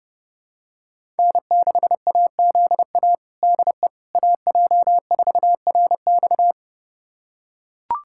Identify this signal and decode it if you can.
Morse code, heard as a keyed tone, about 30 wpm, 700 Hz. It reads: N6AZA DE AJ4RX
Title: USA Callsign DE Callsign - 5 Characters - 30wpm